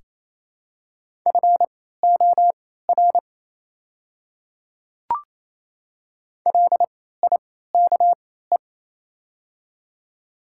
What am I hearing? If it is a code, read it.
Morse code, heard as a keyed tone, about 28 wpm, spaced as for 16 wpm, 700 Hz. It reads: FOR LIKE
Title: 3x Character Spacing - Top 100 Words - 28wpm